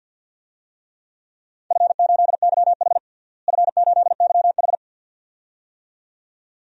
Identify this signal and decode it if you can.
Morse code, heard as a keyed tone, about 46 wpm, 700 Hz. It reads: F8YH F8YH
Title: DXCC-100 Call Signs - 4 Characters - Repeated 2x - 50wpm